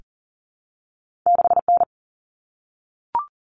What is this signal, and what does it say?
Morse code, heard as a keyed tone, about 40 wpm, 700 Hz. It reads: 6N
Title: Common Cut Numbers - 2-Digit 40wpm